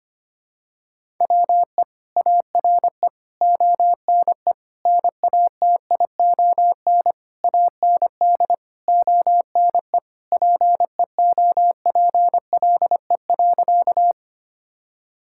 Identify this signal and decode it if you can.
Morse code, heard as a keyed tone, about 25 wpm, 700 Hz. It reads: WE ARE ONE NATION AND ONE PEOPLE.
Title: Sentences from Top 1000 Words - Review Part 6 - 25wpm